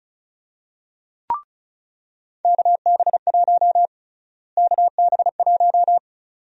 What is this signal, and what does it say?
Morse code, heard as a keyed tone, about 36 wpm, 700 Hz. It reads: KB1 KB1
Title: USA Call Sign Prefixes - Repeated 2x - 35wpm